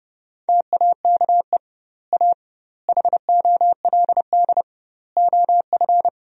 Morse code, heard as a keyed tone, about 30 wpm, 700 Hz. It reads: TAKE A HOLD OF
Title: 2x Word-Spacing - Sentences from Top 400 Words - 30wpm